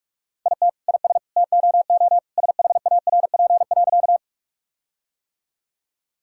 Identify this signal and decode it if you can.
Morse code, heard as a keyed tone, about 46 wpm, 700 Hz. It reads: IT IS TOO SHARP.
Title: Sentences from Top 1600 Words 45wpm